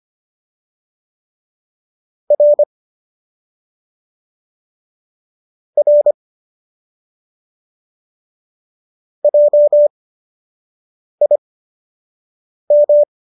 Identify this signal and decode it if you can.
Morse code, heard as a keyed tone, about 25 wpm, spaced as for 6 wpm, 590 Hz. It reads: R R JIM
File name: qso002 06